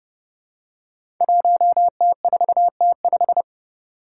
Morse code, heard as a keyed tone, about 30 wpm, 700 Hz. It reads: 1T4T5
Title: Five Challenging Letter-Number Combinations 30wpm